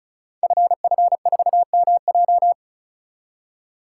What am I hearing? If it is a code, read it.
Morse code, heard as a keyed tone, about 36 wpm, 700 Hz. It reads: FF4MJ
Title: Code Groups - 5 Letter:Numbers - 35wpm